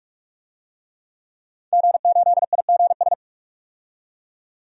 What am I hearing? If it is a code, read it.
Morse code, heard as a keyed tone, about 43 wpm, 700 Hz. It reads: G8IGS
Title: DXCC-100 Call Signs - 5 Characters - 45wpm